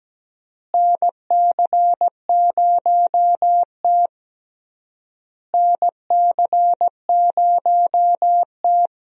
Morse code, heard as a keyed tone, about 17 wpm, 700 Hz. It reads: NC0T NC0T
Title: USA Call Signs - 4 Characters - Repeated 3x - 17wpm